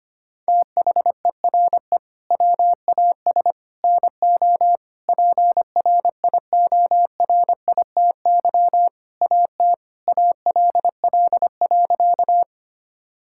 Morse code, heard as a keyed tone, about 25 wpm, 700 Hz. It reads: THERE WAS NO PRIORITY AT ALL.
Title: Sentences from Top 1600 Words 25wpm